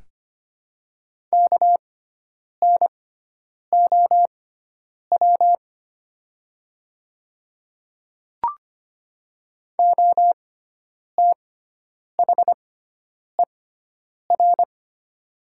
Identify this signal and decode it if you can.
Morse code, heard as a keyed tone, about 25 wpm, spaced as for 9 wpm, 700 Hz. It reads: KNOW OTHER
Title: 6x Character Spacing - Top 200 Words - 25wpm